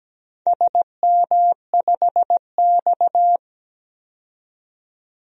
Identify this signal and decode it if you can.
Morse code, heard as a keyed tone, about 17 wpm, 700 Hz. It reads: SM5X
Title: DXCC-100 Call Signs - 4 Characters - 17wpm